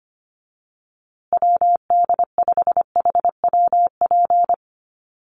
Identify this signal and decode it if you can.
Morse code, heard as a keyed tone, about 25 wpm, 700 Hz. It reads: WD5HWP
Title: USA Call Signs Repeated 2x 25wpm